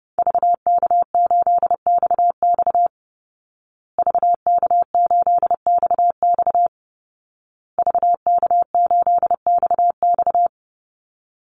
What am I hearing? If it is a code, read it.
Morse code, heard as a keyed tone, about 30 wpm, 700 Hz. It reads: VK8XX VK8XX VK8XX
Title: Call Signs Repeated 2x - 30wpm